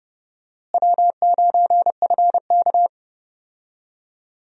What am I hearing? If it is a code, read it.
Morse code, heard as a keyed tone, about 30 wpm, 700 Hz. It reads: W9FK
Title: USA Call Signs - 4 Characters - Repeated 2x - 30wpm